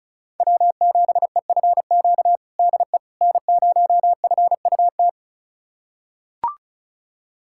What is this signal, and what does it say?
Morse code, heard as a keyed tone, about 36 wpm, 700 Hz. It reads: W7EFQ DE N0FUT
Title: USA Callsign DE Callsign - 5 Characters - 35wpm